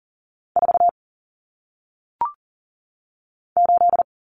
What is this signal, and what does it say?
Morse code, heard as a keyed tone, about 40 wpm, 700 Hz. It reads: 4 8
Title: Common Cut Numbers - 1-Digit 40wpm